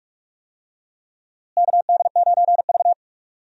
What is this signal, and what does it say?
Morse code, heard as a keyed tone, about 42 wpm, 700 Hz. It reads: KD9V
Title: USA Call Signs - 4 Characters - Repeated 2x - 45wpm